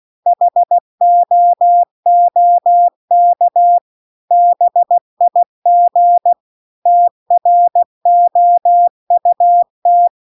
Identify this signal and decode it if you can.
Morse code, heard as a keyed tone, about 16 wpm, 700 Hz. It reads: HOOK BIG TROUT